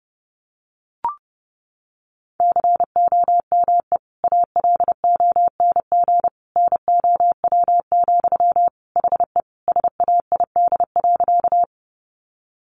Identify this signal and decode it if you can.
Morse code, heard as a keyed tone, about 30 wpm, 700 Hz. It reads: COME ALONG NOW, HE SAID.
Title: Sentences from Top 700 Words 30wpm